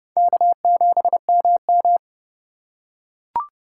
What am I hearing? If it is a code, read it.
Morse code, heard as a keyed tone, about 29 wpm, 700 Hz. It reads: K7MM
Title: USA Call Signs - 4 Characters - 6x Character Spacing - 30wpm